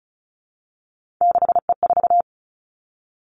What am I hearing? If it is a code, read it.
Morse code, heard as a keyed tone, about 36 wpm, 700 Hz. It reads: 6E4
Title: Three Challenging Letter-Number Combinations 35wpm